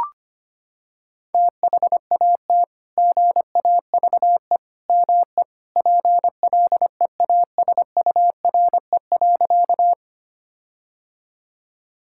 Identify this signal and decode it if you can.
Morse code, heard as a keyed tone, about 25 wpm, 700 Hz. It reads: THAT GAVE ME PLEASURE.
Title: Sentences from Top 1600 Words 25wpm